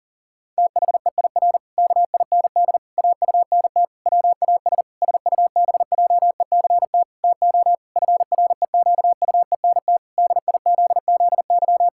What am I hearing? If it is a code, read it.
Morse code, heard as a keyed tone, about 40 wpm, 700 Hz. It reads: THEIR KIND AUNT WAS SUBJECT TO FREQUENT DIZZY
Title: Pangram Challenge v2 - 40wpm